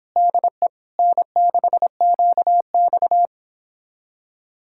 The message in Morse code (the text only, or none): DE N6QX